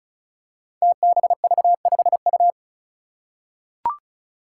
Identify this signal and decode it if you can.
Morse code, heard as a keyed tone, about 34 wpm, 700 Hz. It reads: TBV5U
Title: Five Challenging Letter-Number Combinations 35wpm